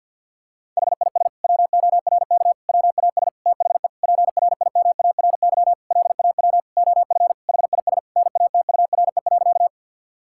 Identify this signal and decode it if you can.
Morse code, heard as a keyed tone, about 52 wpm, 690 Hz. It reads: HIS WORK WAS THE PRIMARY LAW OF HIS NATURE.
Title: Sentences from Top 1000 Words - Review Part 5 - 50wpm